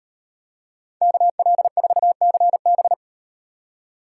Morse code, heard as a keyed tone, about 38 wpm, 700 Hz. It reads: KL4CB